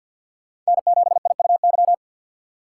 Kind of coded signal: Morse code

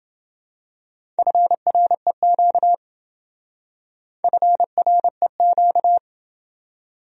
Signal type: Morse code

